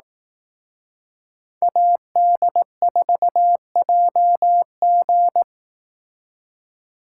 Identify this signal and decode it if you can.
Morse code, heard as a keyed tone, about 18 wpm, 700 Hz. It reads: AD4JG
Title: Speed Racing - USA 2x2 Call Signs 15wpm